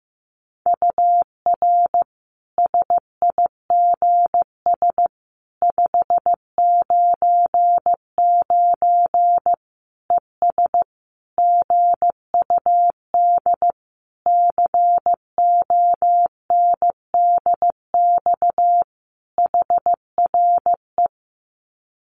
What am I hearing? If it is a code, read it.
Morse code, heard as a keyed tone, about 15 wpm, 700 Hz. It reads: UR SIGS 599 ES GUD CONDX HRE
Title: Common QSO Phrases 15wpm